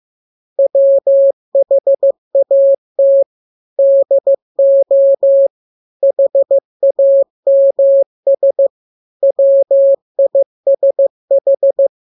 Morse code, heard as a keyed tone, about 15 wpm, 550 Hz. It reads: WHAT DO HAMS WISH
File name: jokes / joke 17 15 0000